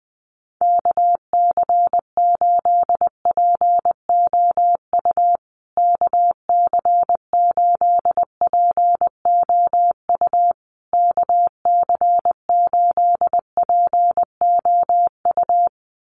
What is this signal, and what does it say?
Morse code, heard as a keyed tone, about 20 wpm, 700 Hz. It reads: KC8POU KC8POU KC8POU
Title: USA Call Signs Repeated 3x 20wpm